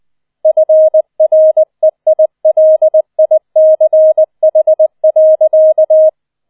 Morse code, heard as a keyed tone, about 20 wpm, 630 Hz. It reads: FREILICH.